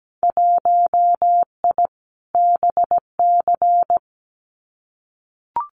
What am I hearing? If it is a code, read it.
Morse code, heard as a keyed tone, about 17 wpm, 700 Hz. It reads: 1I BC
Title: Winter Field Day Exchanges Encore 17wpm